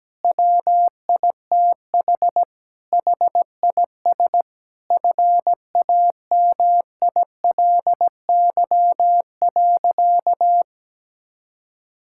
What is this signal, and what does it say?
Morse code, heard as a keyed tone, about 17 wpm, 700 Hz. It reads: WITH HIS FAMILY.
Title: Sentences from Top 1600 Words 17wpm